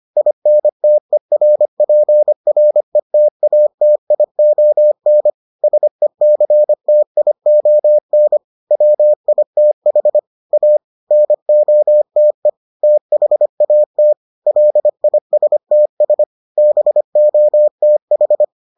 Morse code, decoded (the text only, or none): INTERPRETATION SECTION WITH A NOTE THAT LISTS BOTH